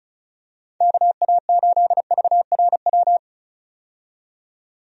Morse code, heard as a keyed tone, about 34 wpm, 700 Hz. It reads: KA8VRW
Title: USA Call Signs - 6 Characters - 35wpm